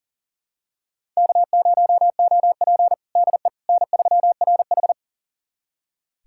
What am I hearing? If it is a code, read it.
Morse code, heard as a keyed tone, about 40 wpm, 700 Hz. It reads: K0OP DE N3RH
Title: USA Callsign DE Callsign - 4 Characters - 40wpm